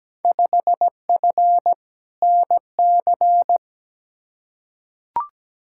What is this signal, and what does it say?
Morse code, heard as a keyed tone, about 17 wpm, 700 Hz. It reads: 5F NC
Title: Field Day Exchanges Encore - 17wpm